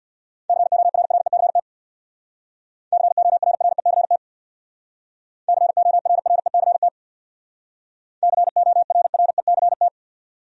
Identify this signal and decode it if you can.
Morse code, heard as a keyed tone, about 67 wpm, 700 Hz. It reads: CORRECT CORRECT CORRECT CORRECT